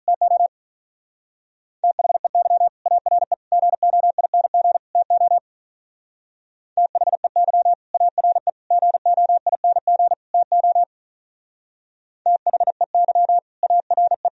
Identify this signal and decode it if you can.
Morse code, heard as a keyed tone, about 42 wpm, 700 Hz. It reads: TO THEY ARE GOING TO THEY ARE GOING TO THEY ARE
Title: Speed-Racing - Sets of 4 Words 35wpm